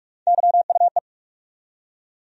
Morse code, heard as a keyed tone, about 42 wpm, 700 Hz. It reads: YUE
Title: USA Call Sign Postfixes - 3 Letters 45wpm